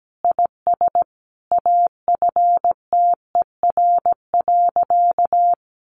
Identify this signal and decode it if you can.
Morse code, heard as a keyed tone, about 17 wpm, 700 Hz. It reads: IS AFTER.
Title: Sentences from Top 1500 Words 17wpm